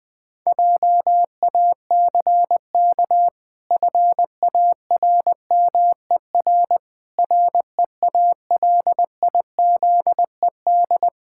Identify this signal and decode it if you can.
Morse code, heard as a keyed tone, about 20 wpm, 700 Hz. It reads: JACK FARMER REALIZED
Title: Pangram Challenge v2 - 20wpm